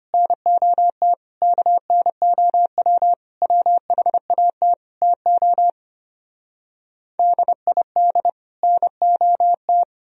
Morse code, heard as a keyed tone, about 33 wpm, 700 Hz. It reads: NOT KNOW WHAT TO DID NOT